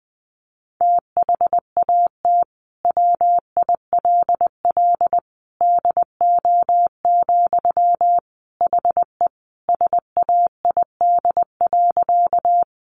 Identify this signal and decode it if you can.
Morse code, heard as a keyed tone, about 20 wpm, 700 Hz. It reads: THAT WILL DO, HE SAID.
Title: Sentences from Top 100 Words 20wpm